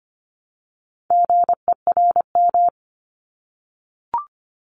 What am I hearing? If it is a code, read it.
Morse code, heard as a keyed tone, about 25 wpm, 700 Hz. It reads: GERM
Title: Multi-letter Phonemes Repeated 3x 25wpm